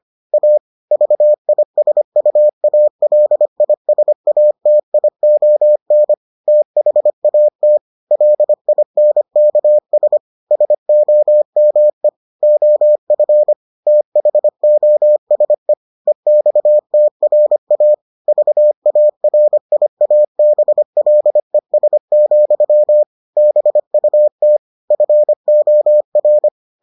Morse code, heard as a keyed tone, about 25 wpm, 600 Hz. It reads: A VISUALISATION THAT LINKS SOME OF THOSE EXTRA VARIABLES, BUT FOR